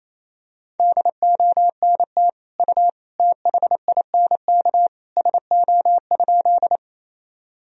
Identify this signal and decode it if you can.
Morse code, heard as a keyed tone, about 28 wpm, 700 Hz. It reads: DONT U THINK SO?
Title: Sentences from Top 300 Words and Abbreviations 28wpm